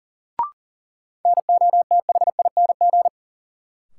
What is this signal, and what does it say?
Morse code, heard as a keyed tone, about 40 wpm, 700 Hz. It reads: NOTHING
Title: Sets of 4 Words - New Word Drill - 40wpm